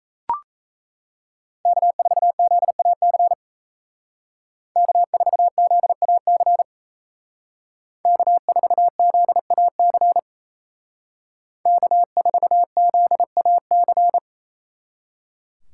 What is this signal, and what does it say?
Morse code, heard as a keyed tone, about 36 wpm, 700 Hz. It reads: K4ZAC K4ZAC K4ZAC K4ZAC